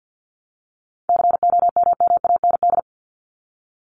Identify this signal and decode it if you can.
Morse code, heard as a keyed tone, about 50 wpm, 700 Hz. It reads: COMMAND
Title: Top 1501-1750 Words 50wpm